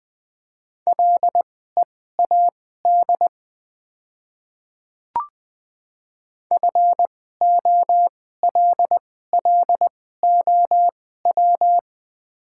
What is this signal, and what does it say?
Morse code, heard as a keyed tone, about 20 wpm, spaced as for 15 wpm, 700 Hz. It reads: LEAD FOLLOW
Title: 2x Character Spacing - Top 500 Words - 20wpm